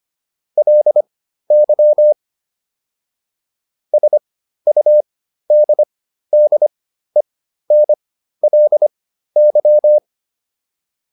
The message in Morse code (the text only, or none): LY SUDDENLY